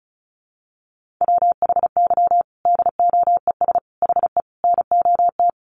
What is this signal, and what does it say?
Morse code, heard as a keyed tone, about 34 wpm, 700 Hz. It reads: WHY DOES HE NOT
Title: Sentences from Top 1400 Words 35wpm